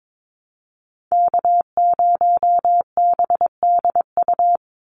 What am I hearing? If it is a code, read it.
Morse code, heard as a keyed tone, about 22 wpm, 700 Hz. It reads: K0BDU
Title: USA Call Signs 22wpm